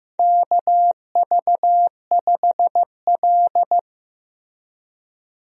Morse code, heard as a keyed tone, about 15 wpm, 700 Hz. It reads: KV5L
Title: USA Call Signs - 4 Characters - Repeated 3x - 15wpm